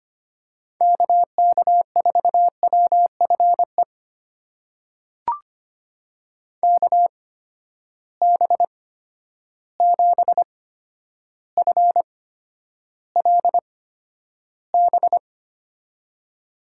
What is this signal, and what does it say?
Morse code, heard as a keyed tone, about 25 wpm, spaced as for 11 wpm, 700 Hz. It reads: KK4WFE KB7FLB
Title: USA Call Signs - 6 Characters - 8x Character Spacing - 25wpm